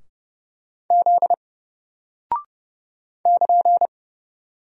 Morse code, heard as a keyed tone, about 30 wpm, 700 Hz. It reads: Z <KN>
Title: Single Character (Letter:Number:Punctuation:Prosign) Alt 30wpm